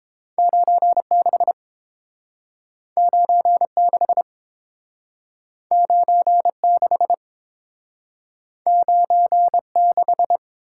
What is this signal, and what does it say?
Morse code, heard as a keyed tone, about 32 wpm, 700 Hz. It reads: 96 96 96 96